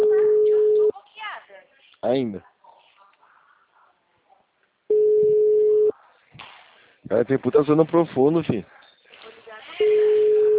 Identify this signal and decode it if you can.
Ringback tone (Europe, 425 Hz), with speech.